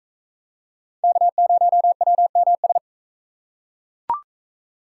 Morse code, heard as a keyed tone, about 41 wpm, 700 Hz. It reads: K0WMS